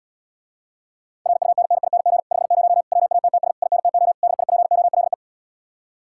Morse code, heard as a keyed tone, about 80 wpm, 700 Hz. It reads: RUNNING 50 WATTS INTO DIPOLE